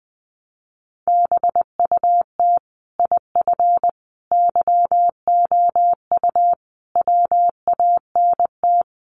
Morse code, heard as a keyed tone, about 20 wpm, 700 Hz. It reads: BUT IF YOU WANT
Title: Sets of 5 Words Encore2 20wpm